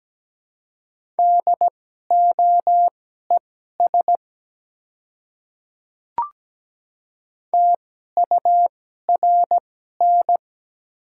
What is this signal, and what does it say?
Morse code, heard as a keyed tone, about 17 wpm, spaced as for 12 wpm, 690 Hz. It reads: DOES TURN
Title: 2x Character Spacing - Top 300 Words - 17wpm